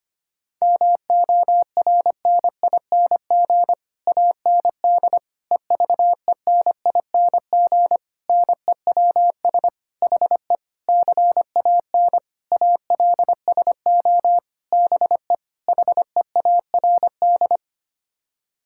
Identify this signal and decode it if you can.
Morse code, heard as a keyed tone, about 25 wpm, 700 Hz. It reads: MORNING AND EVENING NEWS HE CAN ALSO BE HEARD
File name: Sets of 9 Words - 25wpm